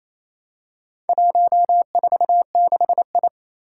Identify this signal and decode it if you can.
Morse code, heard as a keyed tone, about 28 wpm, 700 Hz. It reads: 146I
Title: Four Challenging Letter-Number Combinations 28wpm